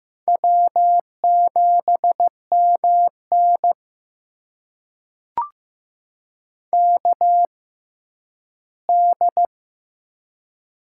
Morse code, heard as a keyed tone, about 15 wpm, spaced as for 10 wpm, 700 Hz. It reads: W7MN KD
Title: USA Call Signs - 4 Characters - 6x Character Spacing - 15wpm